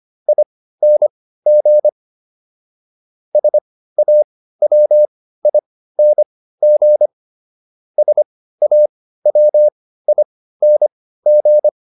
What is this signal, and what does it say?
Morse code, heard as a keyed tone, about 25 wpm, spaced as for 15 wpm, 600 Hz. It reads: ING SAWING SAWING